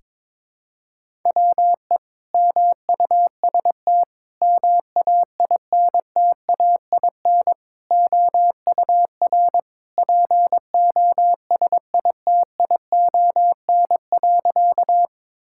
Morse code, heard as a keyed tone, about 22 wpm, 700 Hz. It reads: WE MUST MAINTAIN OUR POSITION.
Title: Sentences from Top 1000 Words - Review Part 5 - 22wpm